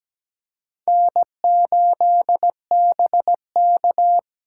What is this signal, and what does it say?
Morse code, heard as a keyed tone, about 17 wpm, 700 Hz. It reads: N8BK
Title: USA Call Signs - 4 Characters - 3x Character Spacing - 17wpm